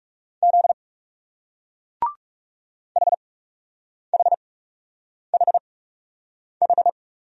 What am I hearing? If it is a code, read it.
Morse code, heard as a keyed tone, about 39 wpm, spaced as for 8 wpm, 700 Hz. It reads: Z HHHH